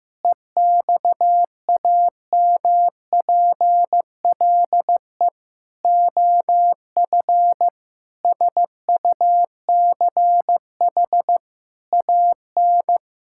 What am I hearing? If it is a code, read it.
Morse code, heard as a keyed tone, about 16 wpm, 700 Hz. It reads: EXAMPLE OF SUCH AN
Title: Sentences from Top 1000 Words - Review Part 5 - 15wpm